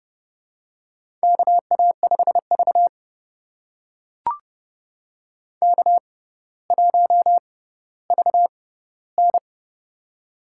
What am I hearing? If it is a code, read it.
Morse code, heard as a keyed tone, about 31 wpm, spaced as for 16 wpm, 700 Hz. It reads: KA5V K1VN
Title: USA Call Signs - 4 Characters - 6x Character Spacing - 30wpm